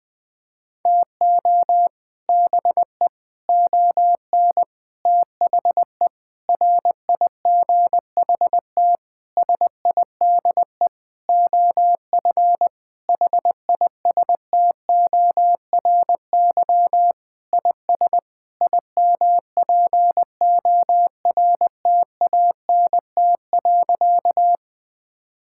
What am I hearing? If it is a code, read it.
Morse code, heard as a keyed tone, about 21 wpm, 700 Hz. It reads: TO BE ON THE RIGHT SIDE OF HISTORY IS IMPORTANT.
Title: Sentences with 8-Word Phrases - Spoken Beginning - 20wpm